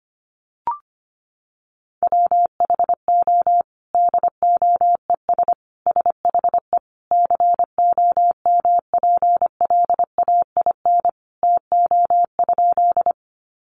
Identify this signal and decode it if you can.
Morse code, heard as a keyed tone, about 25 wpm, 700 Hz. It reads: WHO DOES SHE COMPLAIN TO?
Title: Sentences from Top 1300 Words 25wpm